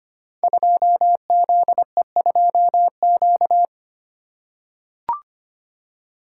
Morse code, heard as a keyed tone, about 25 wpm, 700 Hz. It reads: 2ZE2Q
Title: Code Groups - 5 Letter:Numbers - 25wpm